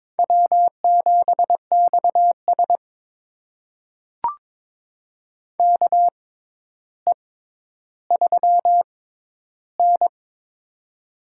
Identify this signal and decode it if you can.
Morse code, heard as a keyed tone, about 21 wpm, spaced as for 11 wpm, 700 Hz. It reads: W7XS KE3N
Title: USA Call Signs - 6x Character Spacing - 22wpm